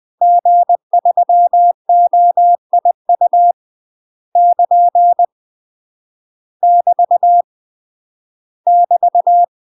Morse code, heard as a keyed tone, about 20 wpm, 700 Hz. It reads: G3OIU <KN> <BT> <BT>
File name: UK MASTER QSOs TRAINING / 20WPM / UK MASTER QSOs 0065